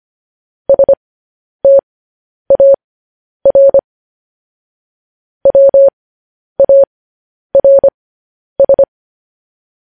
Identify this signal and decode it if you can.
Morse code, heard as a keyed tone, about 25 wpm, spaced as for 10 wpm, 550 Hz. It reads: STAR WARS
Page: news headlines in morse code at 10 wpm